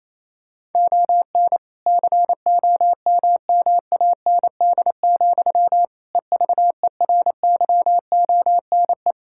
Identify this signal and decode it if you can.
Morse code, heard as a keyed tone, about 28 wpm, 700 Hz. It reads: ON COMMAND, EVERYONE
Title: Sentences from Top 1600 Words 28wpm